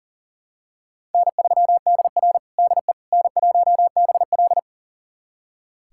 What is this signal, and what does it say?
Morse code, heard as a keyed tone, about 37 wpm, 700 Hz. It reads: N3DR DE N1BL
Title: USA Callsign DE Callsign - 4 Characters - 40wpm